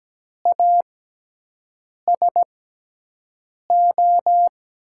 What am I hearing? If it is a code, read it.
Morse code, heard as a keyed tone, about 17 wpm, spaced as for 6 wpm, 700 Hz. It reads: ASO
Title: USA Call Signs - 6x Character Spacing - 17wpm